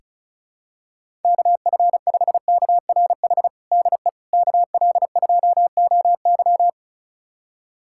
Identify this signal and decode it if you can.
Morse code, heard as a keyed tone, about 36 wpm, 700 Hz. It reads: KF5KRH DE KL2OY
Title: USA Callsign DE Callsign 35wpm